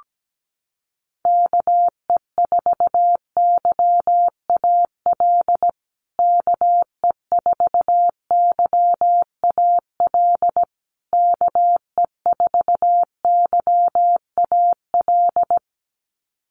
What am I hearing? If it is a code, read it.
Morse code, heard as a keyed tone, about 17 wpm, 700 Hz. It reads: KE4YAL KE4YAL KE4YAL
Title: USA Call Signs Repeated 3x 17wpm